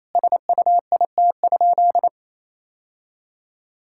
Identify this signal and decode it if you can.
Morse code, heard as a keyed tone, about 28 wpm, 700 Hz. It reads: SUIT?
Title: Sentences from Top 1600 Words 28wpm